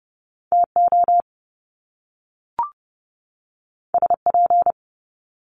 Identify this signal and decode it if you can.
Morse code, heard as a keyed tone, about 30 wpm, 700 Hz. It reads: TO SP
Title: Top Two Letter Combinations in Words 30wpm